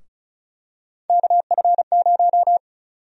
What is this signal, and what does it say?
Morse code, heard as a keyed tone, about 36 wpm, 700 Hz. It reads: KF0
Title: USA Call Sign Prefixes - Repeated 2x - 35wpm